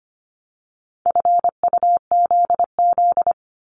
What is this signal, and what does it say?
Morse code, heard as a keyed tone, about 25 wpm, 700 Hz. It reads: FUZZ
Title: Multi-letter Phonemes Repeated 3x 25wpm